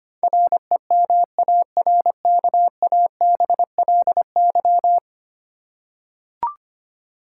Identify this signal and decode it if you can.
Morse code, heard as a keyed tone, about 25 wpm, 690 Hz. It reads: REMARKABLY
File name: 10 Letter Words - First 3 Letters Given - 25wpm